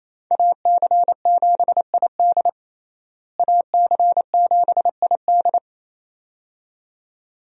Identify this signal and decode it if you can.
Morse code, heard as a keyed tone, about 28 wpm, 700 Hz. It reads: AC7ID AC7ID
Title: USA Call Signs - 5 Characters - Repeated 2x - 28wpm